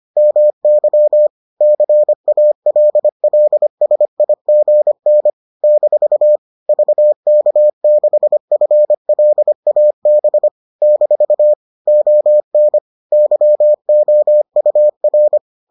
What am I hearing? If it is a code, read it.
Morse code, heard as a keyed tone, about 25 wpm, 600 Hz. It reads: MY CALLSIGN - VK6FLAB - ON YOUR